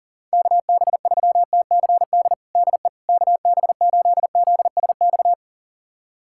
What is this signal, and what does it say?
Morse code, heard as a keyed tone, about 37 wpm, 700 Hz. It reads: KB3TCD DE KB8ZSX
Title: USA Callsign DE Callsign - 6 Characters - 40wpm